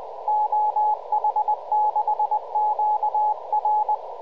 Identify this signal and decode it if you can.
Morse code, heard as a keyed tone, about 20 wpm, 850 Hz, with radio noise. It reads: OH6QR
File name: CQ OH6QR